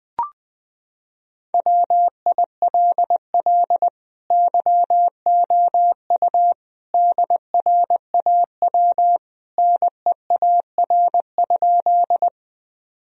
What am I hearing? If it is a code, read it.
Morse code, heard as a keyed tone, about 20 wpm, 700 Hz. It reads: WILL YOU DRAW NEAR?
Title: Sentences from Top 1000 Words - Review Part 1 - 20wpm